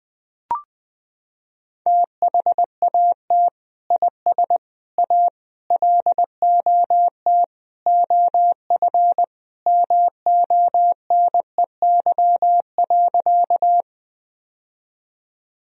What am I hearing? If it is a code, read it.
Morse code, heard as a keyed tone, about 20 wpm, 700 Hz. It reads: THAT IS A LOT OF MONEY.
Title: Sentences from Top 1000 Words - Review Part 3 - 20wpm